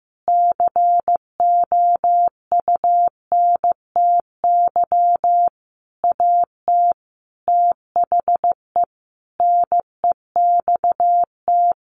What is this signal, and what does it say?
Morse code, heard as a keyed tone, about 15 wpm, 700 Hz. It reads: COUNTY AT THE NEXT
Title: Sentences from Top 1500 Words 15wpm